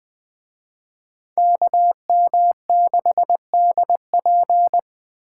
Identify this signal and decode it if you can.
Morse code, heard as a keyed tone, about 20 wpm, 700 Hz. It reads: KM6DP